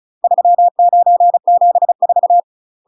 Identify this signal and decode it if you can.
Morse code, heard as a keyed tone, about 36 wpm, 700 Hz. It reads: 3974
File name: UK MASTER QSOs TRAINING / 35WPM / UK MASTER QSOs 0018